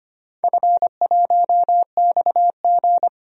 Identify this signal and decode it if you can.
Morse code, heard as a keyed tone, about 25 wpm, 700 Hz. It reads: F1XG
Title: DXCC-100 Call Signs - 4 Characters - 25wpm